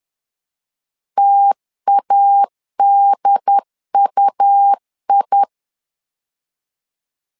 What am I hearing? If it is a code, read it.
Morse code, heard as a keyed tone, about 10 wpm, 800 Hz. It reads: TADUI
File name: LEZIONI / 2 UHDN / 2 UHD 10wpm 1826